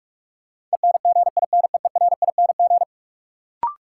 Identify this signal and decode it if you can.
Morse code, heard as a keyed tone, about 47 wpm, 700 Hz. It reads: ENGINEERING
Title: Sentences from Top 1500 Words - New Word Drill Alt - 45wpm